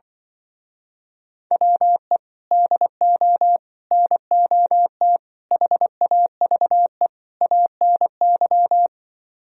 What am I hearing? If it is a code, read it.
Morse code, heard as a keyed tone, about 24 wpm, 700 Hz. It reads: WE DO NOT HAVE ANY
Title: Speed-Racing - Sets of 5 Words 20wpm